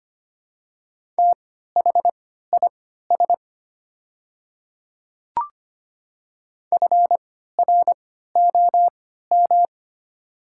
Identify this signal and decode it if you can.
Morse code, heard as a keyed tone, about 25 wpm, spaced as for 14 wpm, 700 Hz. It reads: THIS FROM
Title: 3x Character Spacing - Top 300 Words - 25wpm